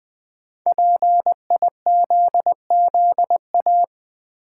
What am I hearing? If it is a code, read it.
Morse code, heard as a keyed tone, about 20 wpm, 700 Hz. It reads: PIZZA
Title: Five Letter Words - 20wpm